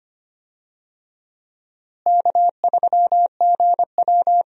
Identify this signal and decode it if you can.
Morse code, heard as a keyed tone, about 25 wpm, 700 Hz. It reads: K3GW
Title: USA Call Signs - 4 Characters - Repeated 3x - 25wpm